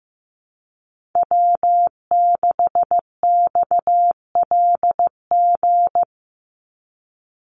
Morse code, heard as a keyed tone, about 15 wpm, 700 Hz. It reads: W6XLG
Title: USA Call Signs 15wpm